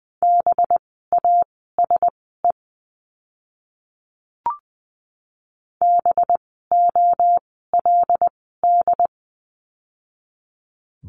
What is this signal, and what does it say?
Morse code, heard as a keyed tone, about 20 wpm, spaced as for 15 wpm, 700 Hz. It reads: BASE BOLD
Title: 185 Lesson 30f - Letter b - Words 20wpm